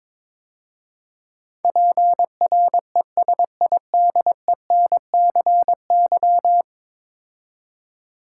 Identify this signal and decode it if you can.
Morse code, heard as a keyed tone, about 22 wpm, 700 Hz. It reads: PRESIDENCY
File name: practice / 22wpm / 10 Letter Words - First 3 Letters Given - 22wpm